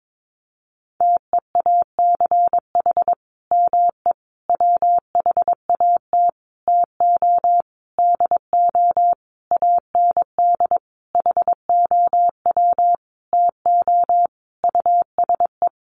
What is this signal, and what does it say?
Morse code, heard as a keyed tone, about 22 wpm, 700 Hz. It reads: TEACH ME WHAT TO DO AND HOW TO USE
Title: Sentences from Top 500 Words 22wpm